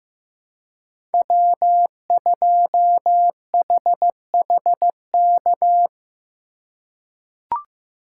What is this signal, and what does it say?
Morse code, heard as a keyed tone, about 15 wpm, 700 Hz. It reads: W2HHK
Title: USA Call Signs - 5 Characters - 2x Character Spacing - 15wpm